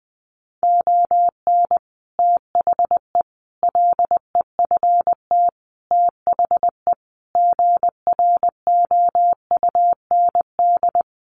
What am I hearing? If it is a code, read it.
Morse code, heard as a keyed tone, about 20 wpm, 700 Hz. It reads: ON THE LEFT THE GROUND
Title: Sentences from Top 700 Words 20wpm